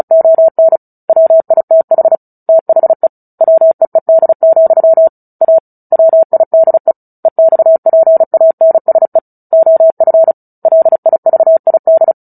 Morse code, heard as a keyed tone, about 34 wpm, 650 Hz. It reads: ON WITH THE WEED, A WIDE EXPANSE OF LIVID
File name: Book Two / 09-Chapter 9 - 35 WPM